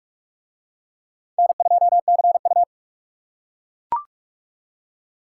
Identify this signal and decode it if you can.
Morse code, heard as a keyed tone, about 39 wpm, 700 Hz. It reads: N2CU